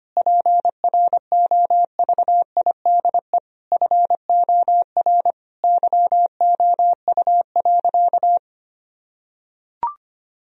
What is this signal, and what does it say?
Morse code, heard as a keyed tone, about 25 wpm, 700 Hz. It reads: PROVIDE FOR YOU.
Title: Sentences from Top 1000 Words - Review Part 3 - 25wpm